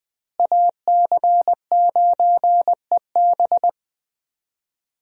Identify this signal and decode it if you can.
Morse code, heard as a keyed tone, about 20 wpm, 700 Hz. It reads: AC9EB
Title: USA Call Signs - 5 Characters - Repeated 3x - 20wpm